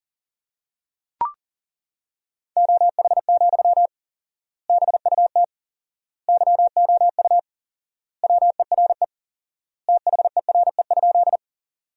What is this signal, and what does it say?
Morse code, heard as a keyed tone, about 40 wpm, 700 Hz. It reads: OH, BUT YOU WERE THERE?
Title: 4x Word-Spacing - Sentences from Top 500 Words - 40wpm